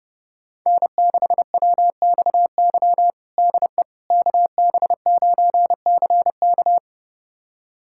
Morse code, heard as a keyed tone, about 32 wpm, 700 Hz. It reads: N6WXY DE KB9CK
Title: USA Callsign DE Callsign - 5 Characters - 30wpm